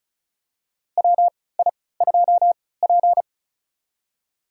Morse code, heard as a keyed tone, about 36 wpm, spaced as for 20 wpm, 700 Hz. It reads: WI2P
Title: USA Call Signs - 4 Characters - 3x Character Spacing - 35wpm